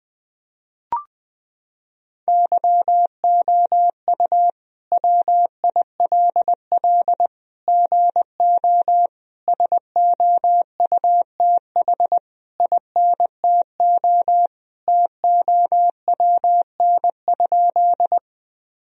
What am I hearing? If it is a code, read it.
Morse code, heard as a keyed tone, about 20 wpm, 700 Hz. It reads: YOU WILL GO SOUTH INTO TOWN?
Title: Sentences from Top 1000 Words - Review Part 2 - 20wpm